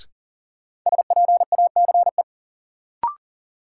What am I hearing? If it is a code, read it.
Morse code, heard as a keyed tone, about 40 wpm, 700 Hz. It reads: SPACE
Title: Sentences from Top 1000 Words - Review Part 5 - New Word Drill - 40wpm